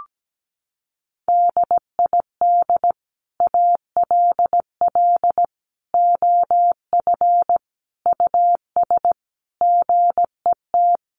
Morse code, heard as a keyed tone, about 17 wpm, 700 Hz. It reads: DID ALL OF US GET
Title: A Sound Of Thunder 17wpm